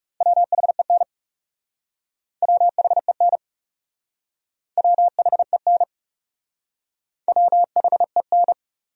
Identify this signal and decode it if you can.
Morse code, heard as a keyed tone, about 38 wpm, 700 Hz. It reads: WHEN WHEN WHEN WHEN